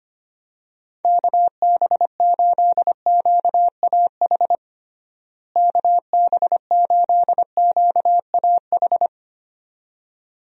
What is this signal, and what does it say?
Morse code, heard as a keyed tone, about 25 wpm, 700 Hz. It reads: KB8QAH KB8QAH
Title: USA Call Signs - 6 Characters - Repeated 2x - 25wpm